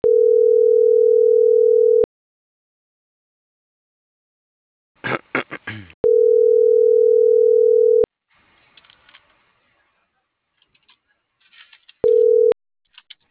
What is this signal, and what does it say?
Ringback tone (North America), 440 + 480 Hz.